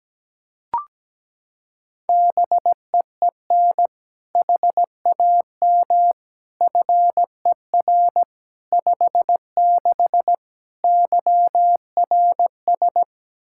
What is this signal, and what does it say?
Morse code, heard as a keyed tone, about 17 wpm, 700 Hz. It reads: BEEN HAM FER 56 YRS
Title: Practice QSO - Exchange 3 - More Info - 17wpm